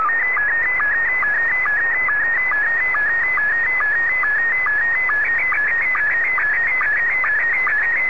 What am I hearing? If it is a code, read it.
SSTV (slow-scan television), Scottie 1.